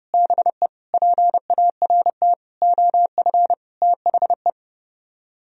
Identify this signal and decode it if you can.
Morse code, heard as a keyed tone, about 30 wpm, 700 Hz. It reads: BE PART OF THE